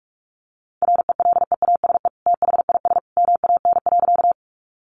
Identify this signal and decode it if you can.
Morse code, heard as a keyed tone, about 44 wpm, 700 Hz. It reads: RELEASE THIS MAN.
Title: Sentences from Top 1100 Words 45wpm